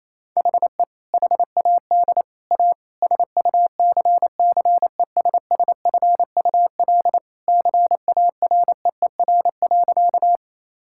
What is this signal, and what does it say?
Morse code, heard as a keyed tone, about 28 wpm, 700 Hz. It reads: HE HAD A SUCCESSFUL CAREER.
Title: Sentences from Top 1000 Words - Review Part 6 - 28wpm